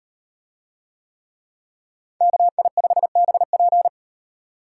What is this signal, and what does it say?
Morse code, heard as a keyed tone, about 37 wpm, 700 Hz. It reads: KI5BP